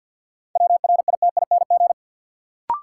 Morse code, heard as a keyed tone, about 51 wpm, 700 Hz. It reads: WRITING
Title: Sentences from Top 800 Words - New Word Drill - 50wpm